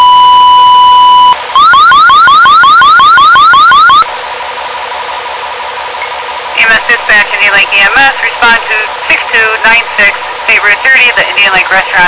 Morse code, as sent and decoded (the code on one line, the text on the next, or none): none